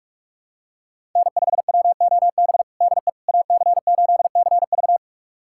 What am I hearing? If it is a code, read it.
Morse code, heard as a keyed tone, about 46 wpm, 700 Hz. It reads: N5WOB DE AC8CV
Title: USA Callsign DE Callsign - 5 Characters - 45wpm